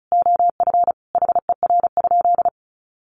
Morse code, heard as a keyed tone, about 34 wpm, 700 Hz. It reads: OF HER?
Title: Sentences from Top 200 Words 35wpm